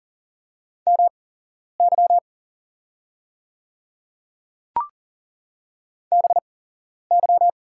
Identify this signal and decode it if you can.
Morse code, heard as a keyed tone, about 40 wpm, spaced as for 11 wpm, 700 Hz. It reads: MY BY
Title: 8x Character Spacing - Top 100 Words - 40wpm